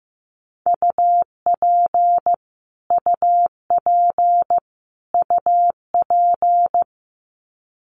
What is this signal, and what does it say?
Morse code, heard as a keyed tone, about 15 wpm, 700 Hz. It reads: UP UP UP
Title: Top 200 Words Repeated 15wpm